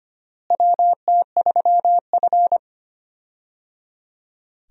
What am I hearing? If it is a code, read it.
Morse code, heard as a keyed tone, about 25 wpm, 700 Hz. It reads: WT3F
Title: USA Call Signs - 4 Characters - Repeated 3x - 25wpm